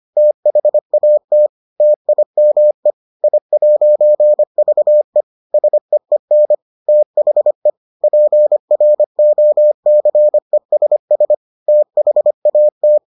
Morse code, heard as a keyed tone, about 25 wpm, 600 Hz. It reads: THAT TIME I'VE SEEN THE PROCESS THAT